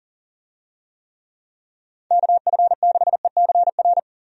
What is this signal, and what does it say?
Morse code, heard as a keyed tone, about 38 wpm, 700 Hz. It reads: KF6ECR
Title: USA Call Signs - 6 Characters - 40wpm